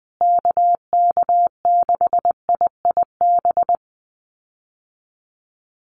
Morse code, heard as a keyed tone, about 20 wpm, 700 Hz. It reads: KK6IIB
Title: USA Call Signs Repeated 2x 20wpm